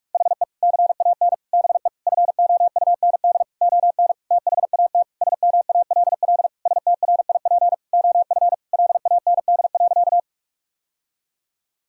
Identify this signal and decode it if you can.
Morse code, heard as a keyed tone, about 46 wpm, 700 Hz. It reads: HE CAN BE FOUND ON THAT SMALL STRIP OF LAND.
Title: Sentences from Top 1600 Words 45wpm